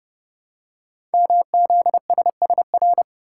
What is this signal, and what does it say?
Morse code, heard as a keyed tone, about 30 wpm, 700 Hz. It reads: MZSSR
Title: Code Groups - 5 Letters - 30wpm